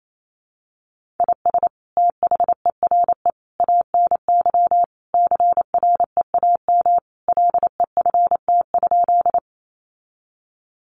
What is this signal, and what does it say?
Morse code, heard as a keyed tone, about 28 wpm, 700 Hz. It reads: IS THERE ANY CREAM LEFT?
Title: Sentences from Top 1500 Words 28wpm